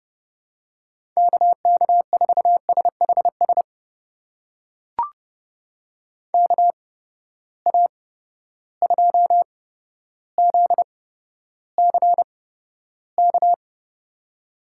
Morse code, heard as a keyed tone, about 30 wpm, spaced as for 13 wpm, 700 Hz. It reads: KK4SHS KA2ZCK
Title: USA Call Signs - 6 Characters - 8x Character Spacing - 30wpm